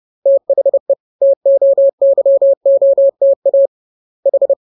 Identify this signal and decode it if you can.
Morse code, heard as a keyed tone, about 30 wpm, 550 Hz. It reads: THE TOYOTA H
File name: jokes / joke 13 30 0000